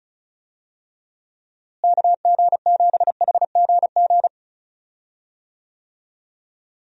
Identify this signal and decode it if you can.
Morse code, heard as a keyed tone, about 34 wpm, 700 Hz. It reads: KG7HGG
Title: USA Call Signs - 6 Characters - Repeated 3x - 35wpm